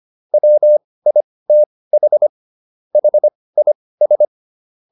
Morse code, heard as a keyed tone, about 25 wpm, spaced as for 18 wpm, 600 Hz. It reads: WITH HIS